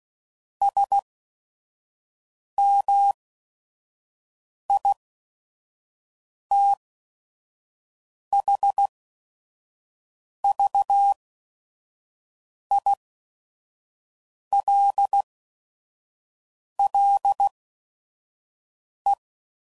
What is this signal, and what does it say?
Morse code, heard as a keyed tone, about 16 wpm, spaced as for 5 wpm, 800 Hz. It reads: SMITHVILLE
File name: CW Code Groups @5WPM / fromSM's / Cities